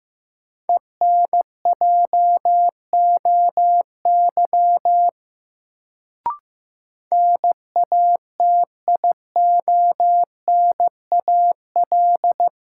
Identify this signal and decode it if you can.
Morse code, heard as a keyed tone, about 15 wpm, 700 Hz. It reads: ENJOY NATIONAL
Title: Sentences from Top 1000 Words - Review Part 4 - New Word Drill - 15wpm